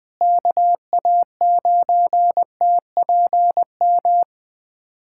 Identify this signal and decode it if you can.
Morse code, heard as a keyed tone, about 20 wpm, 700 Hz. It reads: KA9TPM
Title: USA Callsign DE Callsign - 6 Characters - 20wpm